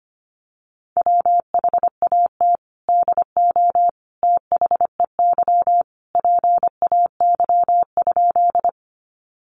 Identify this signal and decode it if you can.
Morse code, heard as a keyed tone, about 25 wpm, 700 Hz. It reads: WHAT DO THEY PAY?
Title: Sentences from Top 700 Words 25wpm